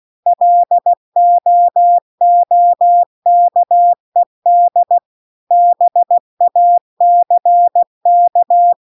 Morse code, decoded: LOOKED BACK

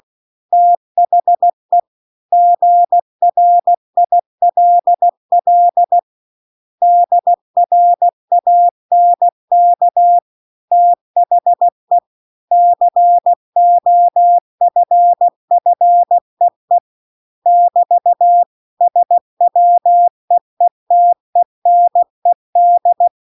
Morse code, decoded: THE GRILL DRANK THE COFFEE <BT> SWEETENED

